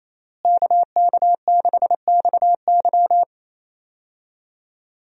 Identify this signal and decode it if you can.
Morse code, heard as a keyed tone, about 28 wpm, 700 Hz. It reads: KK6XY
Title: USA Call Signs - 5 Characters - Repeated 2x - 28wpm